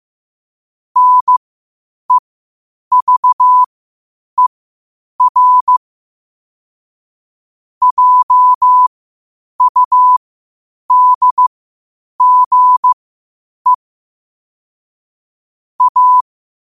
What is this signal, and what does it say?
Morse code, heard as a keyed tone, about 15 wpm, spaced as for 8 wpm, 1000 Hz. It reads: NEVER JUDGE A
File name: qotdfiles / 8wpm